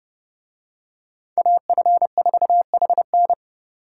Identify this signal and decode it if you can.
Morse code, heard as a keyed tone, about 29 wpm, 700 Hz. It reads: AF4HN